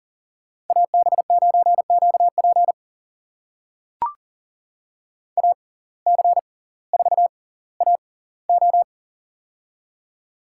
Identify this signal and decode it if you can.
Morse code, heard as a keyed tone, about 40 wpm, spaced as for 20 wpm, 700 Hz. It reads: AB9QP AC4AO
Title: USA Call Signs - 5 Characters - 6x Character Spacing - 40wpm